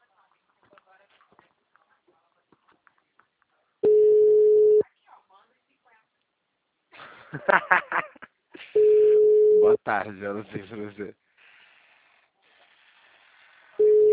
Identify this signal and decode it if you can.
Ringback tone (Europe, 425 Hz), with speech.